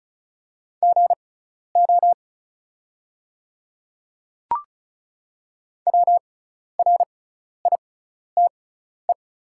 Morse code, heard as a keyed tone, about 36 wpm, spaced as for 12 wpm, 700 Hz. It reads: GO WRITE
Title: 6x Character Spacing - Top 100 Words - 35wpm